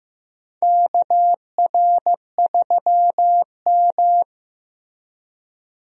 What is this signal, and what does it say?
Morse code, heard as a keyed tone, about 15 wpm, 700 Hz. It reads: KR3M
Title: USA Call Signs - 4 Characters - Repeated 3x - 15wpm